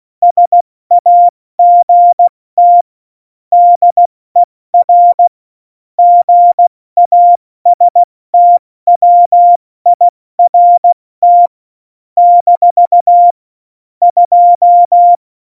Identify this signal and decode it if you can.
Morse code, heard as a keyed tone, about 16 wpm, spaced as for 14 wpm, 700 Hz. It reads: SAGT DER GASTWIRT - 2
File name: Morsekurs / Witze / Witz 15. Oktober 2019, 70 F80